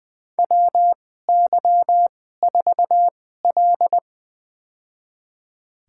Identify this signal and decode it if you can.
Morse code, heard as a keyed tone, about 20 wpm, spaced as for 15 wpm, 700 Hz. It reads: WY4L
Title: USA Call Signs - 4 Characters - 2x Character Spacing - 20wpm